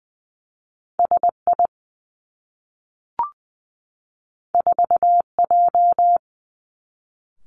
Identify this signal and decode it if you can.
Morse code, heard as a keyed tone, about 19 wpm, 700 Hz. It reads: S I 4 J
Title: Two Challenging Letter-Number Combinations 20wpm